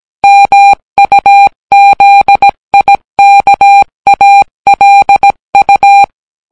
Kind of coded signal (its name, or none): Morse code